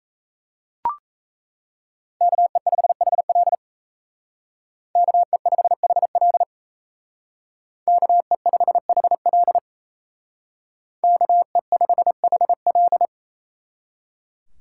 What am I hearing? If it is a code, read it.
Morse code, heard as a keyed tone, about 35 wpm, 700 Hz. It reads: KE5HL KE5HL KE5HL KE5HL